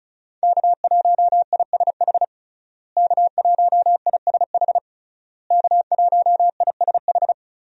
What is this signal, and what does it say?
Morse code, heard as a keyed tone, about 36 wpm, 700 Hz. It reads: K1ISH K1ISH K1ISH
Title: USA Call Signs - 5 Characters - Repeated 3x - 35wpm